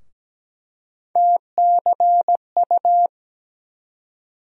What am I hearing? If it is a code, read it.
Morse code, heard as a keyed tone, about 17 wpm, 700 Hz. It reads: TCU
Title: USA Call Sign Postfixes - 3 Letters 17wpm